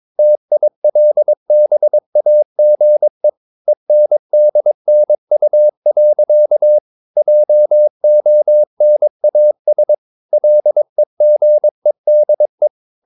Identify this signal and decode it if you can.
Morse code, heard as a keyed tone, about 22 wpm, 600 Hz. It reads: TILBAGE ENDNU. JONAS LEGEDE
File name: Ferie QRS 22wpm